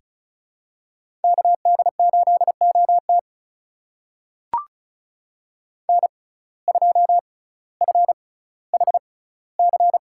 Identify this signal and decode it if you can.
Morse code, heard as a keyed tone, about 36 wpm, spaced as for 18 wpm, 700 Hz. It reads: KD8OT N2FHC
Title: USA Call Signs - 5 Characters - 6x Character Spacing - 35wpm